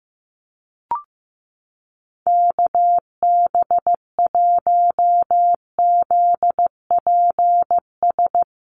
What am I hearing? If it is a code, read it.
Morse code, heard as a keyed tone, about 14 wpm, 700 Hz. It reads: KB1ZPS
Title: USA Call Signs Repeated 2x 15wpm